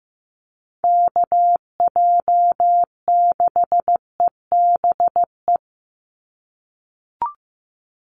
Morse code, heard as a keyed tone, about 15 wpm, 700 Hz. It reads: KJ6EBE
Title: USA Call Signs 15wpm